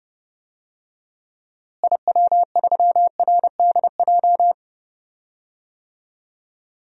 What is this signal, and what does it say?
Morse code, heard as a keyed tone, about 30 wpm, 700 Hz. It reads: IW3RDJ
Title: DXCC-100 Call Signs - 6 Characters - Repeated 2x - 30wpm